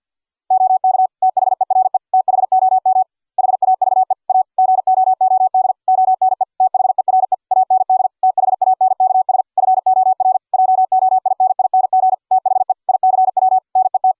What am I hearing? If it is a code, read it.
Morse code, heard as a keyed tone, about 48 wpm, 750 Hz. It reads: OK THERE THOM HAVE A GOOD ONE THERE AND THANKS FOR JOINING THE ICW NET